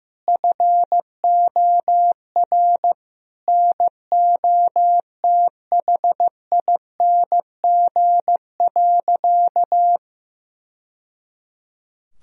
Morse code, heard as a keyed tone, about 15 wpm, 700 Hz. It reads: FOR NOTHING.
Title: Sentences from Top 1000 Words - Review Part 3 - 15wpm